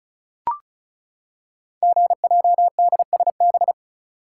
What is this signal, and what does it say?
Morse code, heard as a keyed tone, about 36 wpm, 700 Hz. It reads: GJDSB
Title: Code Groups - 5 Letters - 35wpm